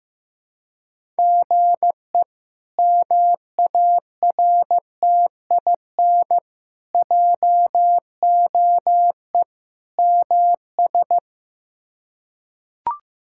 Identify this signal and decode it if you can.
Morse code, heard as a keyed tone, about 15 wpm, 700 Hz. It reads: GE MARTIN JOE MS
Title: K1USN SST QSO Practice - 15wpm